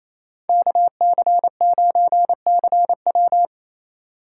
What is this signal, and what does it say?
Morse code, heard as a keyed tone, about 28 wpm, 700 Hz. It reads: KC9CW
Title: USA Call Signs - 5 Characters - 2x Character Spacing - 28wpm